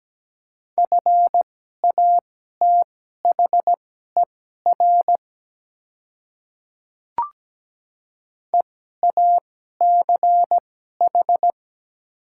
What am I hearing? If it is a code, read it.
Morse code, heard as a keyed tone, about 17 wpm, spaced as for 12 wpm, 700 Hz. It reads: FATHER EACH
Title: 2x Character Spacing - Top 200 Words - 17wpm